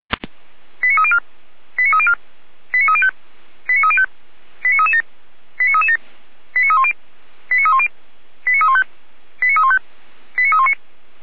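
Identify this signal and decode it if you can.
Five-tone selective call (ZVEI), heard as quick runs of five tones, tones of about 70 ms. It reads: 89304 89304 89305 89305 89337 89337 89310 89310 89315 89315 89319
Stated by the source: ZVEI five-tone selective call received on an SDR, from the Feuerwehr Orlamünde fire brigade audio server.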